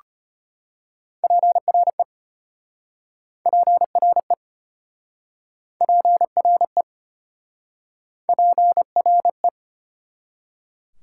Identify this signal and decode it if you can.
Morse code, heard as a keyed tone, about 34 wpm, 700 Hz. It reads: PRE PRE PRE PRE